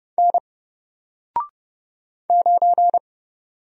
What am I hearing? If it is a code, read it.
Morse code, heard as a keyed tone, about 30 wpm, 700 Hz. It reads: N 9
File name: Single-Letter-Number-30wpm